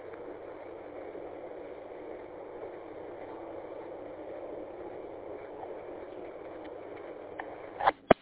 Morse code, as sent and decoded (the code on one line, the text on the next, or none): none